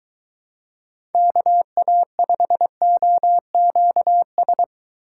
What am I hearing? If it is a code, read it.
Morse code, heard as a keyed tone, about 23 wpm, 700 Hz. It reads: KA5OQS